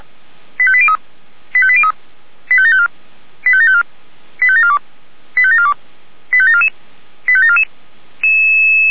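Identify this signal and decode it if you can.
Five-tone selective call (ZVEI), heard as quick runs of five tones, tones of about 70 ms. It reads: 85803 85803 85754 85754 85742 85742 85744 85744
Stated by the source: ZVEI five-tone selective call received with an SDR.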